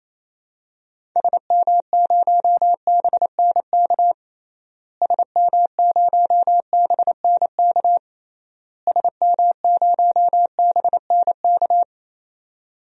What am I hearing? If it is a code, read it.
Morse code, heard as a keyed tone, about 28 wpm, 700 Hz. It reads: SM0BNK SM0BNK SM0BNK
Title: DXCC-100 Call Signs - 6 Characters - Repeated 3x - 28wpm